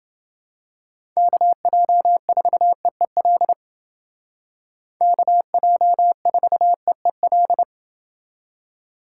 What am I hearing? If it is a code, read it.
Morse code, heard as a keyed tone, about 28 wpm, 700 Hz. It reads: KJ4EEL KJ4EEL